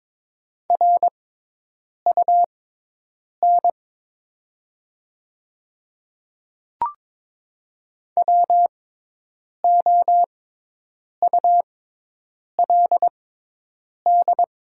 Morse code, heard as a keyed tone, about 22 wpm, spaced as for 8 wpm, 700 Hz. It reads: RUN WOULD
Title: 6x Character Spacing - Top 500 Words - 22wpm